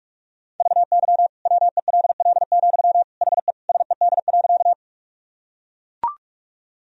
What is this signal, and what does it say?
Morse code, heard as a keyed tone, about 42 wpm, 700 Hz. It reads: VY WELL, HE SED.
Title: Sentences from Top 200 Words and Abbreviations 45wpm